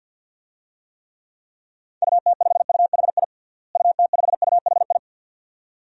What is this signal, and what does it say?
Morse code, heard as a keyed tone, about 45 wpm, 700 Hz. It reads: UT5UHI UT5UHI
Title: DXCC-100 Call Signs - 6 Characters - Repeated 2x - 50wpm